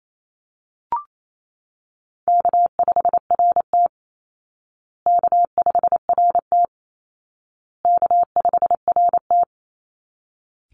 Morse code, heard as a keyed tone, about 28 wpm, 700 Hz. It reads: K5RT K5RT K5RT
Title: Easy Call Signs Repeated 3x - 28wpm